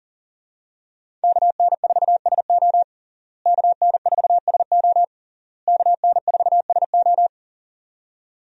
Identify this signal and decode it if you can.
Morse code, heard as a keyed tone, about 40 wpm, 700 Hz. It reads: KN4SO KN4SO KN4SO
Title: USA Call Signs - 5 Characters - Repeated 3x - 40wpm